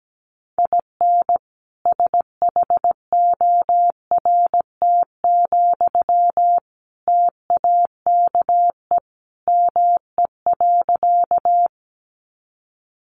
Morse code, heard as a keyed tone, about 17 wpm, 700 Hz. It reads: IN SHORT, TAKE ME.
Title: Sentences from Top 400 Words 17wpm